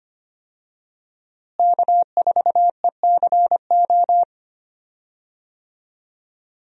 Morse code, heard as a keyed tone, about 25 wpm, 700 Hz. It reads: K4ECO
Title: USA Call Signs - 5 Characters - Repeated 3x - 25wpm